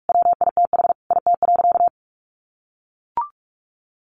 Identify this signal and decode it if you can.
Morse code, heard as a keyed tone, about 44 wpm, 700 Hz. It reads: WITH IT.
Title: Sentences from Top 1400 Words 45wpm